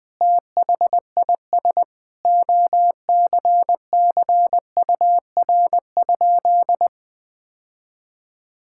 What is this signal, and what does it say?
Morse code, heard as a keyed tone, about 20 wpm, 700 Hz. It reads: THIS OCCUR?
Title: Sentences from Top 1000 Words - Review Part 6 - 20wpm